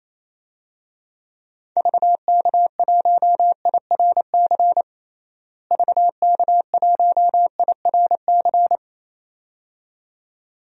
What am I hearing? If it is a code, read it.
Morse code, heard as a keyed tone, about 28 wpm, 700 Hz. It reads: VK1IRC VK1IRC
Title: DXCC-100 Call Signs - 6 Characters - Repeated 2x - 28wpm